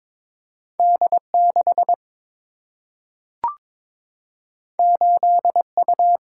Morse code, heard as a keyed tone, about 22 wpm, 700 Hz. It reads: D 6 8 U
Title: Single-Letter Single-Number 22wpm